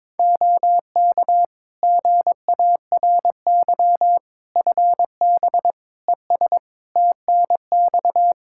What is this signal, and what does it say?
Morse code, heard as a keyed tone, about 22 wpm, 700 Hz. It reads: OK GARY FB ES TNX
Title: Practice QSO - Exchange 2 - Info - 22wpm